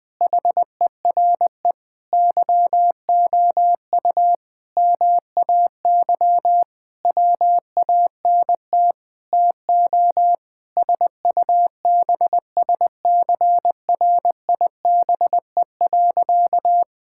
Morse code, heard as a keyed tone, about 21 wpm, 700 Hz. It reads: HERE YOU MAY WANT TO SUBSCRIBE.
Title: Sentences with 8-Word Phrases - 20wpm